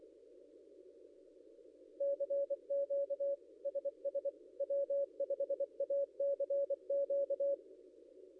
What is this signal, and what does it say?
Morse code, heard as a keyed tone, about 25 wpm, 560 Hz. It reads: CQ SS W5ACQ
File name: contests / ss-cw-k5tr-op-n5rz -0800